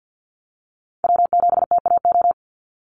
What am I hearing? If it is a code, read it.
Morse code, heard as a keyed tone, about 50 wpm, 700 Hz. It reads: W7TAO
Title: NA SOTA Chaser Callsigns 50wpm